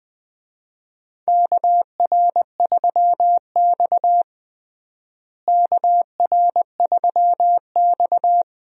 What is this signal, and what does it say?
Morse code, heard as a keyed tone, about 20 wpm, 700 Hz. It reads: KR3X KR3X
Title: USA Call Signs - 4 Characters - Repeated 2x - 20wpm